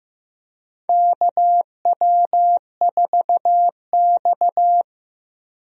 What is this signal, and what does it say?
Morse code, heard as a keyed tone, about 15 wpm, 700 Hz. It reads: KW4X
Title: USA Call Signs - 4 Characters - Repeated 2x - 15wpm